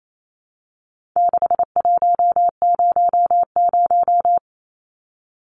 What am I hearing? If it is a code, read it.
Morse code, heard as a keyed tone, about 28 wpm, 700 Hz. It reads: 6100
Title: Numbers - 4-Digit 28wpm